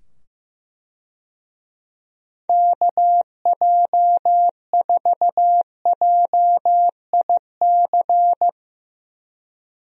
Morse code, heard as a keyed tone, about 15 wpm, 700 Hz. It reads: KJ4JIC
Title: USA Call Signs - 6 Characters - 15wpm